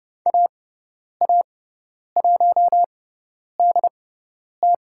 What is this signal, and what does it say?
Morse code, heard as a keyed tone, about 30 wpm, spaced as for 10 wpm, 700 Hz. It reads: AA1DT
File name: Lesson 17g - Letter u - Call Signs 30wpm